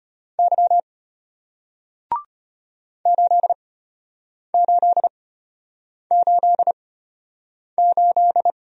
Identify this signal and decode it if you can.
Morse code, heard as a keyed tone, about 33 wpm, spaced as for 8 wpm, 700 Hz. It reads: Y 8888